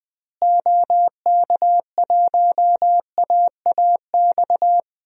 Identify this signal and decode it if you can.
Morse code, heard as a keyed tone, about 20 wpm, 700 Hz. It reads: OK1AAX
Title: DXCC-100 Call Signs - 6 Characters - Repeated 2x - 20wpm